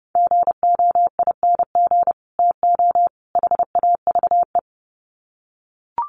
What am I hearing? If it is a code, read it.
Morse code, heard as a keyed tone, about 30 wpm, 700 Hz. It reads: GOING TO HAVE
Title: Sets of 3 Words 30wpm